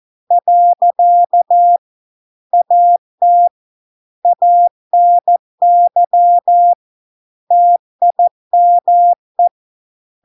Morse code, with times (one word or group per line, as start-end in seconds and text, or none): 0.3-1.8 .
2.5-3.5 AT
4.3-6.7 ANY
7.5-9.5 TIME